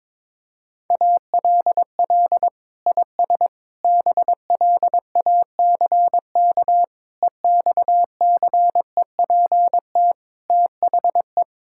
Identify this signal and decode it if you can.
Morse code, heard as a keyed tone, about 22 wpm, 700 Hz. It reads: ALL IS BLACK EXCEPT THE
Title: Sentences from Top 1600 Words 22wpm